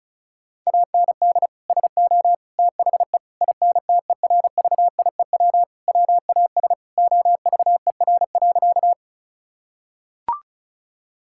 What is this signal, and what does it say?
Morse code, heard as a keyed tone, about 36 wpm, 700 Hz. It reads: AND SO THE INTERVIEW WAS OVER.
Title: Sentences from Top 1600 Words 35wpm